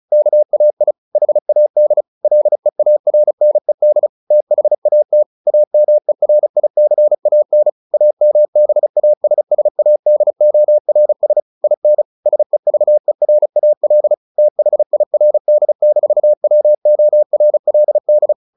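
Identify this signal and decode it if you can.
Morse code, heard as a keyed tone, about 34 wpm, 600 Hz. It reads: KAI HAD LEARNED THAT AMERICAN AMBASSADORS IN SEVERAL THIRD-WORLD
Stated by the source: Morse practice at 35 wpm from CW med Gnister, sending Ken Follett's novel Never in English.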